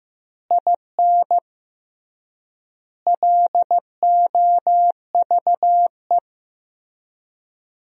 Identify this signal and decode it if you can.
Morse code, heard as a keyed tone, about 15 wpm, 700 Hz. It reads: IN LOVE
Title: 3x Word-Spacing - Sentences from Top 400 Words - 15wpm